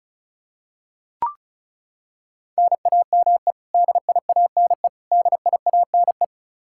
Morse code, heard as a keyed tone, about 36 wpm, 700 Hz. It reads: NAME DIANE DIANE
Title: Practice QSO - Exchange 1 - Beginning - 35wpm